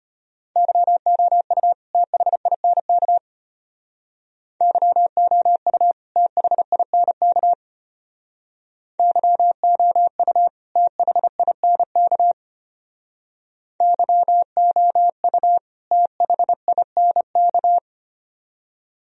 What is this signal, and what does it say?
Morse code, heard as a keyed tone, about 32 wpm, 700 Hz. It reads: YOU THINK YOU THINK YOU THINK YOU THINK